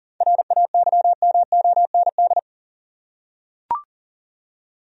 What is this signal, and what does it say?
Morse code, heard as a keyed tone, about 40 wpm, 700 Hz. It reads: RAYMOND
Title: USA Common Names - 40wpm